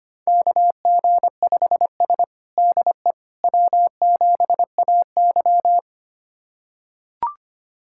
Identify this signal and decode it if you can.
Morse code, heard as a keyed tone, about 25 wpm, 700 Hz. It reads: KG5S DE W7AY
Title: USA Callsign DE Callsign - 4 Characters - 25wpm